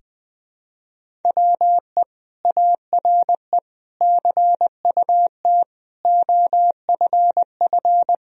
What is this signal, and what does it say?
Morse code, heard as a keyed tone, about 20 wpm, 700 Hz. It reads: WE ARE CUT OFF
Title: Sentences from Top 1000 Words - Review Part 1 - 20wpm